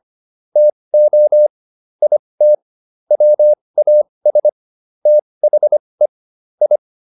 Morse code, heard as a keyed tone, about 25 wpm, spaced as for 20 wpm, 600 Hz. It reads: TO IT WAS THE I